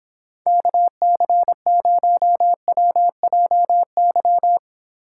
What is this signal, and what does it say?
Morse code, heard as a keyed tone, about 29 wpm, 700 Hz. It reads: KC0WJY